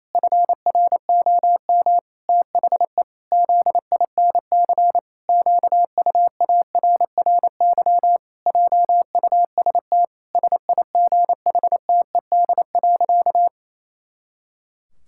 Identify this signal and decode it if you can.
Morse code, heard as a keyed tone, about 28 wpm, 700 Hz. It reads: FROM THE ZINC QUARRY JUST SIGHTED.
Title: Pangram Challenge v2 - 28wpm